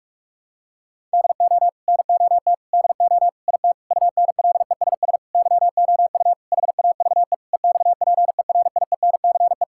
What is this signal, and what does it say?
Morse code, heard as a keyed tone, about 46 wpm, 700 Hz. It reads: DO NOT DO IT UNLESS YOU HAVE EXPERIENCE
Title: Sentences from Top 1000 Words - Review Part 6 - 45wpm